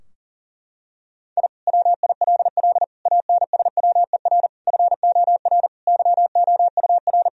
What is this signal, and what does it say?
Morse code, heard as a keyed tone, about 40 wpm, 700 Hz. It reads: I WILL ANSWER FOR YOUR
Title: Sentences from Top 1000 Words - Review Part 4 - 40wpm